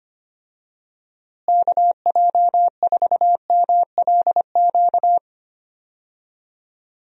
Morse code, heard as a keyed tone, about 25 wpm, 700 Hz. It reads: KJ4MLQ
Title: USA Call Signs - 6 Characters - Repeated 3x - 25wpm